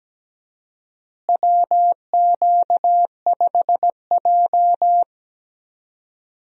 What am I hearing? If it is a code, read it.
Morse code, heard as a keyed tone, about 17 wpm, 700 Hz. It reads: WQ5J
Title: USA Call Signs - 4 Characters - Repeated 2x - 17wpm